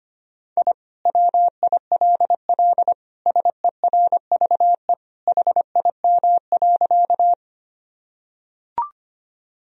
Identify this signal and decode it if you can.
Morse code, heard as a keyed tone, about 25 wpm, 700 Hz. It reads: I WILL SERVE HIM.
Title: Sentences from Top 1000 Words - Review Part 2 - 25wpm